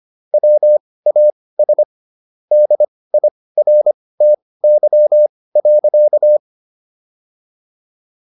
Morse code, heard as a keyed tone, about 25 wpm, spaced as for 18 wpm, 600 Hz. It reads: WAS DIRTY.